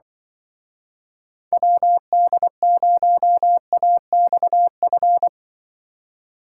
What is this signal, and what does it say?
Morse code, heard as a keyed tone, about 24 wpm, 700 Hz. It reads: WD0AXF
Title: Speed Racing - USA Call Signs 20wpm